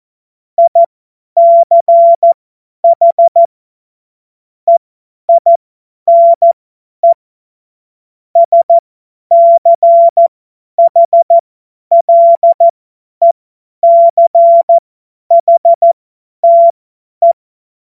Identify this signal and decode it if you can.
Morse code, heard as a keyed tone, about 14 wpm, spaced as for 10 wpm, 680 Hz. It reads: ICH EINE SCHLECHTE
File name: Morsekurs / Witze / Witz 12 März 2024 50 F70